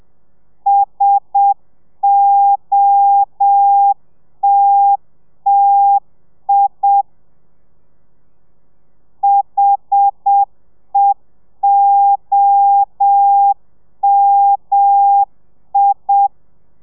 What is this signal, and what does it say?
Morse code, heard as a keyed tone, about 7 wpm, 800 Hz. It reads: SOTTI HEOMI
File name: telegrafia / i2tfj / Lez1 30